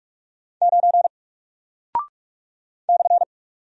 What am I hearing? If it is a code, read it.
Morse code, heard as a keyed tone, about 45 wpm, 700 Hz. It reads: 9 /
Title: Single Character (Letter:Number:Punctuation:Prosign) Alt 45wpm